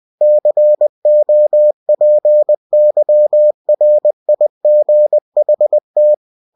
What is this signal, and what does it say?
Morse code, heard as a keyed tone, about 20 wpm, 600 Hz. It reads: COPYRIGHT
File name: cw-books / winnie-the-pooh / winnie-20wpm-page-0112